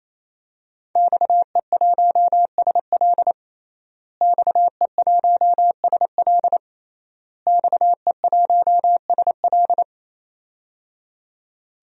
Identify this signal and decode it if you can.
Morse code, heard as a keyed tone, about 27 wpm, 700 Hz. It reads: XE1SL XE1SL XE1SL
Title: DXCC-100 Call Signs - 5 Characters - Repeated 3x - 28wpm